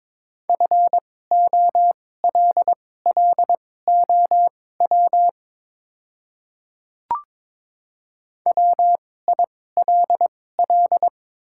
Morse code, heard as a keyed tone, about 22 wpm, spaced as for 16 wpm, 700 Hz. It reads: FOLLOW WILL
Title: 2x Character Spacing - Top 300 Words - 22wpm